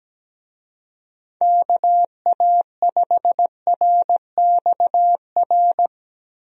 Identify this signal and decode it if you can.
Morse code, heard as a keyed tone, about 17 wpm, 700 Hz. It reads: KA5RXR
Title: USA Call Signs - 6 Characters - Repeated 3x - 17wpm